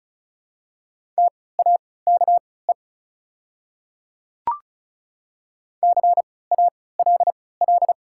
Morse code, heard as a keyed tone, about 34 wpm, spaced as for 20 wpm, 700 Hz. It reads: TAKE CALL
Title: 3x Character Spacing - Top 300 Words - 35wpm